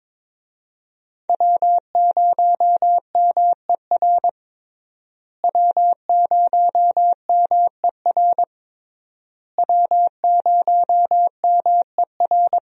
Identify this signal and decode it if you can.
Morse code, heard as a keyed tone, about 22 wpm, 700 Hz. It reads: W0MER W0MER W0MER
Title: USA Call Signs - 5 Characters - Repeated 3x - 22wpm